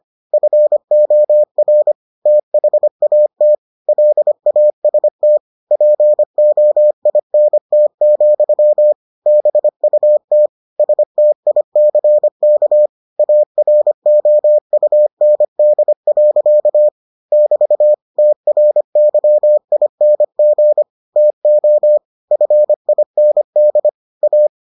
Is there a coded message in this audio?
Yes: Morse code from 0.3 to 24.5 s, heard as a keyed tone, about 25 wpm, 600 Hz. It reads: FOR THAT LAST POINT, BUT STICK AROUND. <BT> TRYING TO FIND A